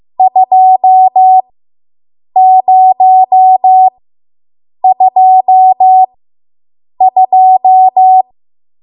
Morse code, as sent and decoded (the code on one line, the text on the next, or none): ..--- ----- ..--- ..---
2022